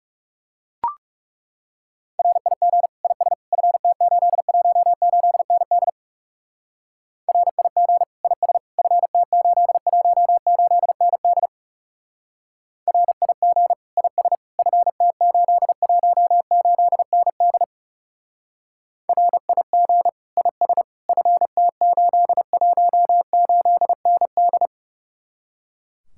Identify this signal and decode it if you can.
Morse code, heard as a keyed tone, about 36 wpm, 700 Hz. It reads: RIG IS FT818ND RIG IS FT818ND RIG IS FT818ND RIG IS FT818ND
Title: Speed-Racing Common QSO Phrases - 30wpm